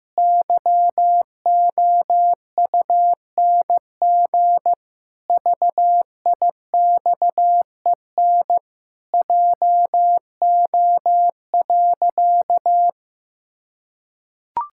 Morse code, heard as a keyed tone, about 15 wpm, 700 Hz. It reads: YOUNG VIXEN JO.
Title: Pangram Challenge v2 - 15wpm